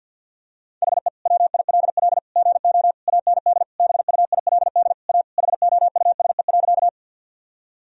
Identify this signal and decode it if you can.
Morse code, heard as a keyed tone, about 49 wpm, 700 Hz. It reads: HE WILL GO AND BUILD A HOUSE.
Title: Sentences from Top 1000 Words - Review Part 1 - 50wpm